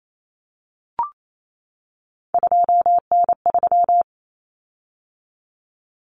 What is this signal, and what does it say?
Morse code, heard as a keyed tone, about 28 wpm, 700 Hz. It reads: 2N3
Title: Common Cut Numbers - 3-Digit 28wpm